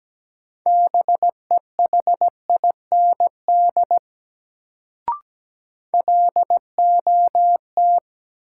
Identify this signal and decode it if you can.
Morse code, heard as a keyed tone, about 17 wpm, 700 Hz. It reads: BEHIND LOT
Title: Sentences from Top 500 Words - New Word Drill - 17wpm